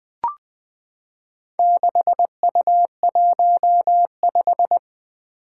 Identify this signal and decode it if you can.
Morse code, heard as a keyed tone, about 20 wpm, 700 Hz. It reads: 6U15
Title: Four Challenging Letter-Number Combinations 20wpm